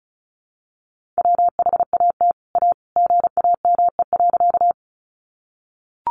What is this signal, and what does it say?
Morse code, heard as a keyed tone, about 34 wpm, 700 Hz. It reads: WHAT A GAME.
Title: Sentences from Top 500 Words 35wpm